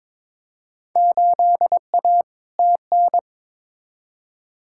Morse code, heard as a keyed tone, about 22 wpm, 700 Hz. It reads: 8A TN
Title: Field Day Exchanges Encore - 22wpm